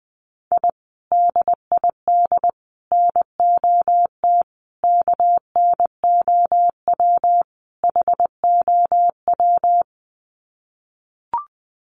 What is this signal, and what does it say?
Morse code, heard as a keyed tone, about 20 wpm, 700 Hz. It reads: I DID NOT KNOW HOW
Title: Sets of 5 Words Encore 20wpm